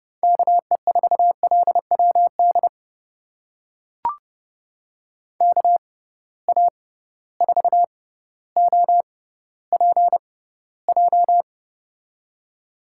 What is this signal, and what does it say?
Morse code, heard as a keyed tone, about 32 wpm, spaced as for 16 wpm, 700 Hz. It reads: KE4LWD KA4OPJ
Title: USA Call Signs - 6 Characters - 6x Character Spacing - 30wpm